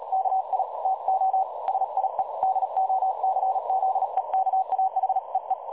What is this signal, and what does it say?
Morse code, heard as a keyed tone, about 38 wpm, 760 Hz, with noise. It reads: KI TO SIE DOCZEKA HEE
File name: dlazaawansowanych / nagraniazpasma / websdr recording 3565.0kHz